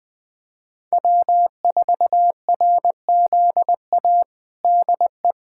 Morse code, heard as a keyed tone, about 20 wpm, 700 Hz. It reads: W4RZA DE
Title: Practice QSO - Exchange 4 - Ending - 20wpm